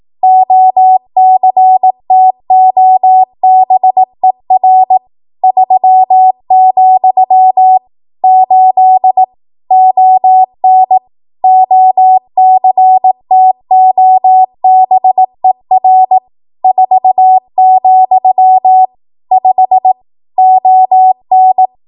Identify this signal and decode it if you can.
Morse code, heard as a keyed tone, about 18 wpm, 750 Hz. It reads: OCTOBER 3, 8 ON OCTOBER 4, 5 ON
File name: Archive / 18 WPM / 18WPM